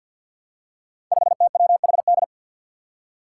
Morse code, heard as a keyed tone, about 47 wpm, 700 Hz. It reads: 5TWHD
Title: Five Challenging Letter-Number Combinations 50wpm